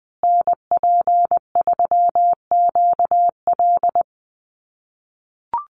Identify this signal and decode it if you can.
Morse code, heard as a keyed tone, about 20 wpm, 700 Hz. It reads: NP3QL
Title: USA Call Signs Repeated 3x 20wpm